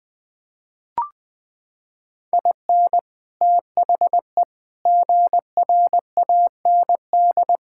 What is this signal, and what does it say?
Morse code, heard as a keyed tone, about 21 wpm, 700 Hz. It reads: IN THE GRAND
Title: Sets of 6 Words - 20wpm